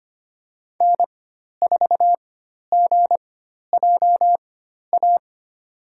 Morse code, heard as a keyed tone, about 25 wpm, spaced as for 12 wpm, 700 Hz. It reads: N4GJA
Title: USA Call Signs - 5 Characters - 4x Character Spacing - 25wpm